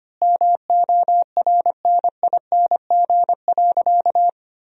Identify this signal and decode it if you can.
Morse code, heard as a keyed tone, about 25 wpm, 700 Hz. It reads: MORNING.
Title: 3x Word-Spacing - Sentences from Top 400 Words - 25wpm